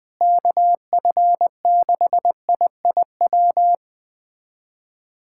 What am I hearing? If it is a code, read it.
Morse code, heard as a keyed tone, about 20 wpm, 700 Hz. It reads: KF6IIW
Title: USA Callsign DE Callsign 20wpm